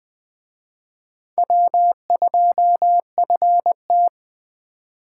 Morse code, heard as a keyed tone, about 20 wpm, 700 Hz. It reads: W2FT
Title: Speed Racing - USA 1x2 Call Signs 17wpm